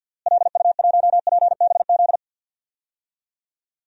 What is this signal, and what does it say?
Morse code, heard as a keyed tone, about 47 wpm, 700 Hz. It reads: LU1PBZ
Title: DXCC-100 Call Signs - 6 Characters - Repeated 2x - 50wpm